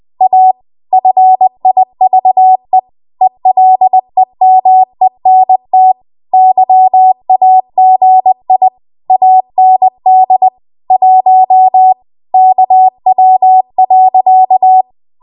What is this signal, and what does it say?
Morse code, heard as a keyed tone, about 20 wpm, 750 Hz. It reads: A FIVE ELEMENT YAGI AND 1 KW.
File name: Morse / 20WPM